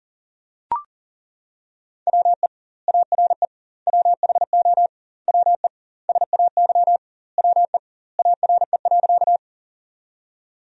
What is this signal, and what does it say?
Morse code, heard as a keyed tone, about 40 wpm, 700 Hz. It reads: WE ARE WHO WE SAY WE ARE.
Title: 2x Word-Spacing - Sentences from Top 200 Words - 40wpm